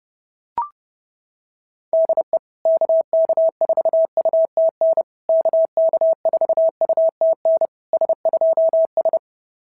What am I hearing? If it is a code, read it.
Morse code, heard as a keyed tone, about 30 wpm, 690 Hz. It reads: DE KK4UTN KK4UTN S2S
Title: SOTA QSO Practice 30wpm